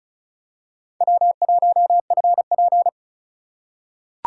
Morse code, heard as a keyed tone, about 36 wpm, 700 Hz. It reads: W1FP
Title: USA Call Signs - 4 Characters - 2x Character Spacing - 35wpm